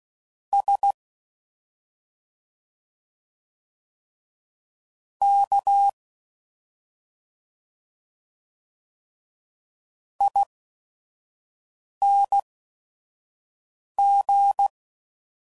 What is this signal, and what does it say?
Morse code, heard as a keyed tone, about 16 wpm, spaced as for 5 wpm, 800 Hz. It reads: S K ING